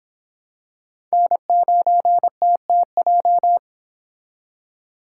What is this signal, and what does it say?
Morse code, heard as a keyed tone, about 26 wpm, 700 Hz. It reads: N9TTJ